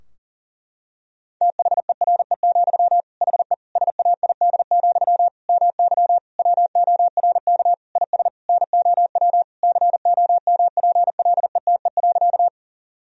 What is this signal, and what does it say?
Morse code, heard as a keyed tone, about 40 wpm, 700 Hz. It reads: THERE, HE SAID, MY WORK IS NOW COMPLETE.
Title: Sentences from Top 1000 Words - Review Part 2 - 40wpm